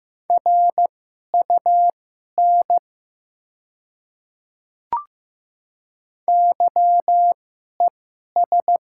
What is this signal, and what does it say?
Morse code, heard as a keyed tone, about 15 wpm, spaced as for 11 wpm, 700 Hz. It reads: RUN YES
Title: 2x Character Spacing - Top 500 Words - 15wpm